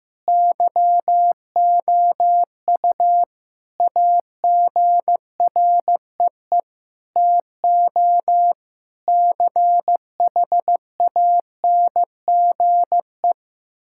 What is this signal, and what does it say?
Morse code, heard as a keyed tone, about 16 wpm, 700 Hz. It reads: YOU AGREE TO CHANGE
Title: Sentences from Top 1000 Words - Review Part 6 - 15wpm